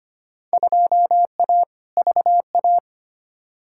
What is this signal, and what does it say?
Morse code, heard as a keyed tone, about 25 wpm, 700 Hz. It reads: 2A VA
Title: Field Day Exchanges - 25wpm